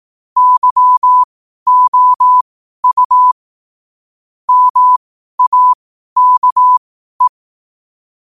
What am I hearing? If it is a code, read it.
Morse code, heard as a keyed tone, about 18 wpm, spaced as for 12 wpm, 1000 Hz. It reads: YOU MAKE
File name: qotdfiles / 12wpm